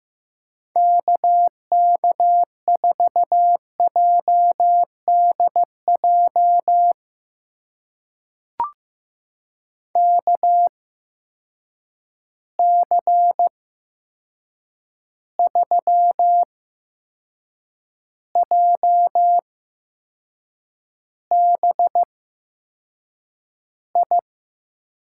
Morse code, heard as a keyed tone, about 15 wpm, spaced as for 6 wpm, 700 Hz. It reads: KK4JDJ KC3JBI
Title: USA Call Signs - 6 Characters - 8x Character Spacing - 15wpm